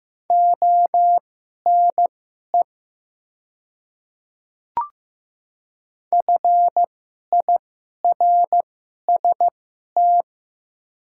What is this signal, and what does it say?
Morse code, heard as a keyed tone, about 15 wpm, spaced as for 11 wpm, 690 Hz. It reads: ONE FIRST
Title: 2x Character Spacing - Top 100 Words - 15wpm